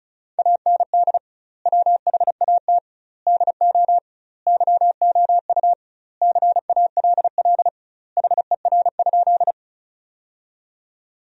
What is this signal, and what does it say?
Morse code, heard as a keyed tone, about 34 wpm, 700 Hz. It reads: AND WHAT DO YOU CALL HER?
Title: 2x Word-Spacing - Sentences from Top 100 Words - 35wpm